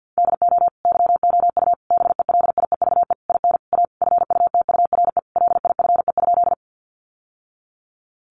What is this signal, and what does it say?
Morse code, heard as a keyed tone, about 50 wpm, 700 Hz. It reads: DO YOU BELIEVE IN A FUTURE LIFE?
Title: Sentences from Top 1000 Words 50wpm